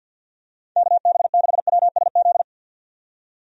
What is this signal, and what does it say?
Morse code, heard as a keyed tone, about 50 wpm, 700 Hz. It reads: KB6WSZ
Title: USA Call Signs - 6 Characters - 50wpm